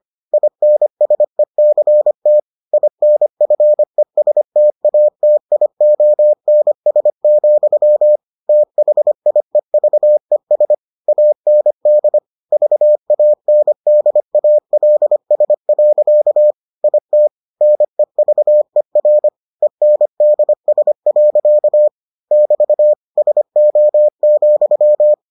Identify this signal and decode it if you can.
Morse code, heard as a keyed tone, about 25 wpm, 600 Hz. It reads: INSECT INFESTATIONS, THIEVES AND VANDALS. IT NEVER ENDS. <BT> SO,